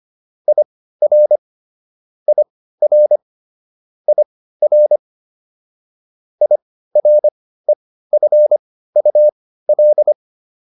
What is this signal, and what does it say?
Morse code, heard as a keyed tone, about 25 wpm, spaced as for 15 wpm, 600 Hz. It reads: IR IR IR IREFUL